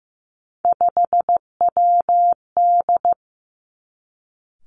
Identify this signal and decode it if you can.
Morse code, heard as a keyed tone, about 15 wpm, 700 Hz. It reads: 5WD
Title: Three Challenging Letter-Number Combinations 15wpm